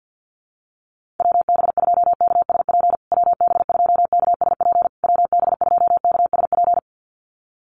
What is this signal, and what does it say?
Morse code, heard as a keyed tone, about 48 wpm, 700 Hz. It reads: WB2KSP WB2KSP WB2KSP
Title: USA Call Signs Repeated 3x 50wpm